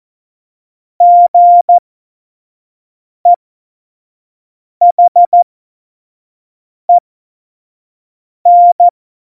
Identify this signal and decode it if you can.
Morse code, heard as a keyed tone, about 14 wpm, spaced as for 5 wpm, 700 Hz. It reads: GEHEN